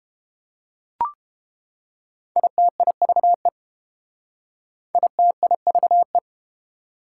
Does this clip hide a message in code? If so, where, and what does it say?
Yes: Morse code from 2.4 to 6.2 s, heard as a keyed tone, about 32 wpm, 700 Hz. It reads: ITIVE ITIVE